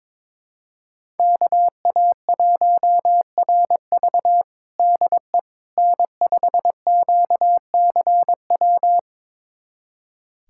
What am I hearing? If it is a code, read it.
Morse code, heard as a keyed tone, about 22 wpm, 700 Hz. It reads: KA1RV DE N5QCW
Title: USA Callsign DE Callsign - 5 Characters - 22wpm